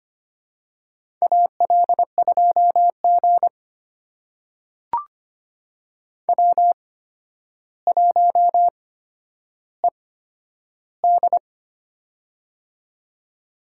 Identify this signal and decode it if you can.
Morse code, heard as a keyed tone, about 25 wpm, spaced as for 11 wpm, 700 Hz. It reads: AL2G W1ED
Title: USA Call Signs - 4 Characters - 8x Character Spacing - 25wpm